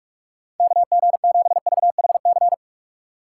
Morse code, heard as a keyed tone, about 59 wpm, 700 Hz. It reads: KG7VHC